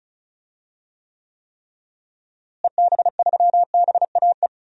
Morse code, heard as a keyed tone, about 34 wpm, 700 Hz. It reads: EB3BAE
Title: DXCC-100 Call Signs - 6 Characters - Repeated 3x - 35wpm